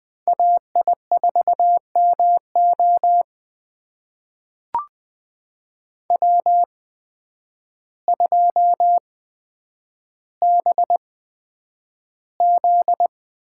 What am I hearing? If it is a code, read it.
Morse code, heard as a keyed tone, about 20 wpm, spaced as for 9 wpm, 700 Hz. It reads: AI4MO W2BZ
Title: USA Call Signs - 8x Character Spacing - 20wpm